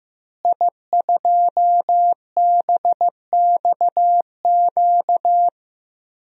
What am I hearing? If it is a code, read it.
Morse code, heard as a keyed tone, about 15 wpm, 700 Hz. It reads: I2BXQ
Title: DXCC-100 Call Signs - 5 Characters - Repeated 3x - 15wpm